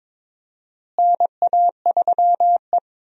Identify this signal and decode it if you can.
Morse code, heard as a keyed tone, about 22 wpm, 700 Hz. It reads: NA3E
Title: USA Call Signs - 4 Characters - Repeated 3x - 22wpm